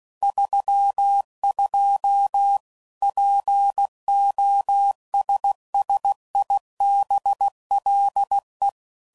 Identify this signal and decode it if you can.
Morse code, heard as a keyed tone, about 16 wpm, 800 Hz. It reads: 32 POSSIBLE
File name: CW Code Groups @15WPM / Technical Subjects / AND Gate